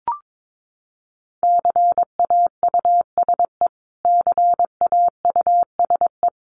Morse code, heard as a keyed tone, about 22 wpm, 700 Hz. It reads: CAUSE CAUSE
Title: Top 300 Words Repeated 22wpm